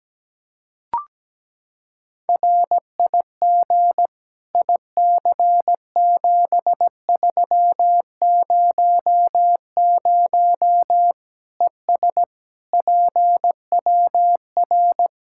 Morse code, heard as a keyed tone, about 17 wpm, 700 Hz. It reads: RIG IC7300 ES PWR
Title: Practice QSO - Exchange 2 - Info - 17wpm